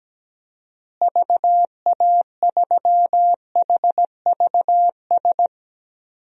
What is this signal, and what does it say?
Morse code, heard as a keyed tone, about 17 wpm, 700 Hz. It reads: VA3HVS
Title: DXCC-100 Call Signs - 6 Characters - Repeated 2x - 17wpm